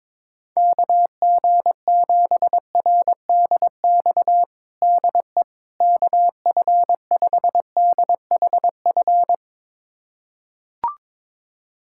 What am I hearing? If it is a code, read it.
Morse code, heard as a keyed tone, about 22 wpm, 700 Hz. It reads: KG7RDX DE KF5DHF
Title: USA Callsign DE Callsign - 6 Characters - 22wpm